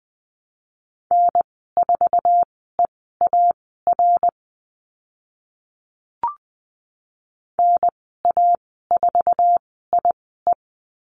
Morse code, heard as a keyed tone, about 20 wpm, spaced as for 14 wpm, 700 Hz. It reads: N4EAR NA4IE
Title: 076 Lesson 13g - Letter L - Call Signs 20wpm